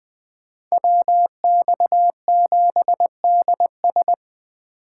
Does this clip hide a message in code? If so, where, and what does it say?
Yes: Morse code from 0.7 to 4.1 s, heard as a keyed tone, about 20 wpm, 700 Hz. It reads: WX7DS